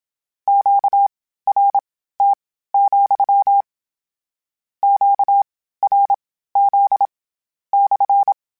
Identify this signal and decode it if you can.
Morse code, heard as a keyed tone, about 27 wpm, spaced as for 15 wpm, 800 Hz. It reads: QRT, QRZ /
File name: Q Calls / Q Calls-mp3s / Q CALLS SPSL @25